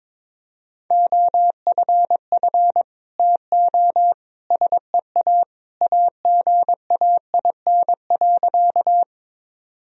Morse code, heard as a keyed tone, about 22 wpm, 700 Hz. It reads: OFF TO SEA AGAIN.
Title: Sentences from Top 1000 Words - Review Part 1 - 22wpm